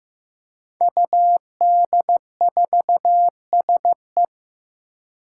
Morse code, heard as a keyed tone, about 15 wpm, 700 Hz. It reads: UD4SE
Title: Five Challenging Letter-Number Combinations 15wpm